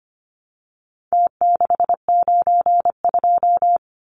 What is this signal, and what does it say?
Morse code, heard as a keyed tone, about 25 wpm, 700 Hz. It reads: T692
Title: Numbers - Contest Style - 4-Digit 25wpm